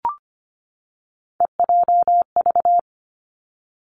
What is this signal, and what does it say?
Morse code, heard as a keyed tone, about 25 wpm, 700 Hz. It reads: EJV
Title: Three Challenging Letter-Number Combinations 25wpm